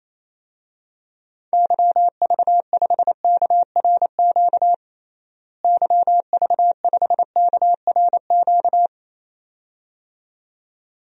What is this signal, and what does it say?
Morse code, heard as a keyed tone, about 28 wpm, 700 Hz. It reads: YV5KRQ YV5KRQ
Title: DXCC-100 Call Signs - 6 Characters - Repeated 2x - 28wpm